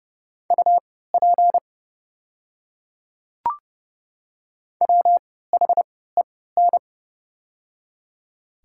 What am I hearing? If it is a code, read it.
Morse code, heard as a keyed tone, about 30 wpm, spaced as for 17 wpm, 700 Hz. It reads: UP WHEN
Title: 3x Character Spacing - Top 100 Words - 30wpm